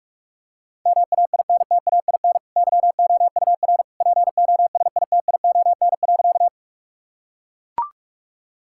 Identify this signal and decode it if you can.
Morse code, heard as a keyed tone, about 41 wpm, 700 Hz. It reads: MAINTAIN YOUR POSITION.
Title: Sentences from Top 1000 Words - Review Part 5 - 45wpm